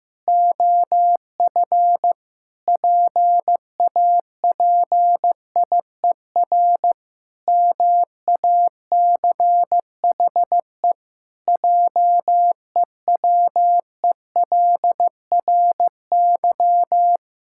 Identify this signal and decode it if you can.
Morse code, heard as a keyed tone, about 14 wpm, 700 Hz. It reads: OF PAPIER MACHE JEWELRY
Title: Pangram Challenge v2 - 15wpm